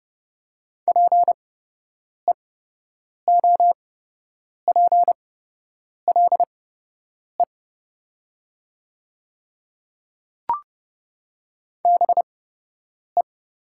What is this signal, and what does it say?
Morse code, heard as a keyed tone, about 30 wpm, spaced as for 8 wpm, 700 Hz. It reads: PEOPLE BE
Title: 8x Character Spacing - Top 100 Words - 30wpm